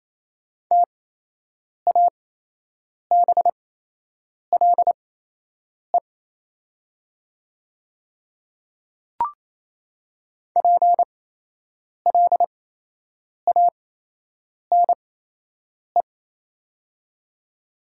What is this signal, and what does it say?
Morse code, heard as a keyed tone, about 27 wpm, spaced as for 8 wpm, 700 Hz. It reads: TABLE PLANE
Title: 8x Character Spacing - Top 500 Words - 28wpm